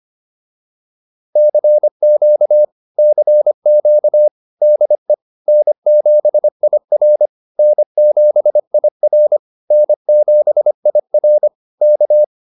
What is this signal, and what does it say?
Morse code, heard as a keyed tone, about 25 wpm, 600 Hz. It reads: CQ CQ DE N7IR N7IR N7IR K